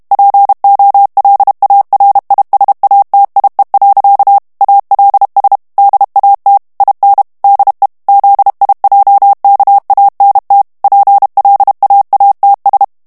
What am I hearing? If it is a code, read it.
Morse code, heard as a keyed tone, about 32 wpm, 800 Hz. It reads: POLARISATIE. ALS DAT IN DE ZIJKANT PLAATS
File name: HOLLANDAIS texte clair 1 vitesse 30 mots minute